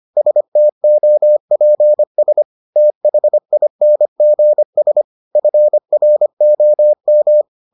Morse code, heard as a keyed tone, about 25 wpm, 600 Hz. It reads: STOPS THINGS FROM